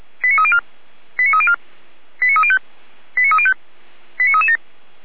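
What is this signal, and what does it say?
Five-tone selective call (ZVEI), heard as quick runs of five tones, tones of about 70 ms. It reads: 89304 89304 89305 89305 89337
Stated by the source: ZVEI five-tone call 89246 received by SDR.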